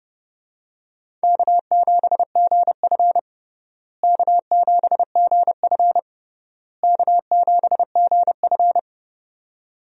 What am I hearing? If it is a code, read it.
Morse code, heard as a keyed tone, about 31 wpm, 700 Hz. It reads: K7GF K7GF K7GF
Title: USA Call Signs - 4 Characters - Repeated 3x - 30wpm